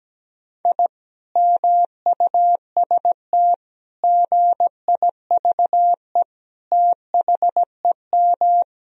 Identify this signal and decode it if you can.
Morse code, heard as a keyed tone, about 17 wpm, 700 Hz. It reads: I MUST GIVE THEM
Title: Sentences from Top 1000 Words - Review Part 2 - 17wpm